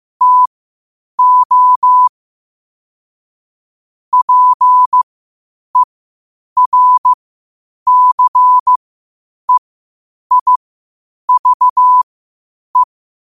Morse code, heard as a keyed tone, about 15 wpm, spaced as for 8 wpm, 1000 Hz. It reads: TO PERCEIVE
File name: qotdfiles / 8wpm